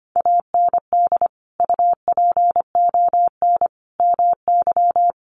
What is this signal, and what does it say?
Morse code, heard as a keyed tone, about 25 wpm, 700 Hz. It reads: AND UPON MY
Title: Sentences from Top 1100 Words 25wpm